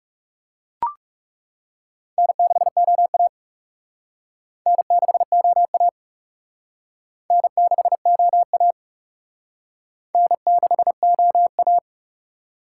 Morse code, heard as a keyed tone, about 38 wpm, 700 Hz. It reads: N6OA N6OA N6OA N6OA